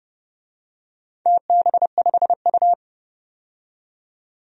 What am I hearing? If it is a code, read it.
Morse code, heard as a keyed tone, about 30 wpm, 700 Hz. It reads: TB5U
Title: DXCC-100 Call Signs - 4 Characters - 30wpm